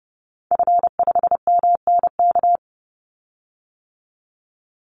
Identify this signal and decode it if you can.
Morse code, heard as a keyed tone, about 30 wpm, 700 Hz. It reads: F5MNK
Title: Call Signs - 30wpm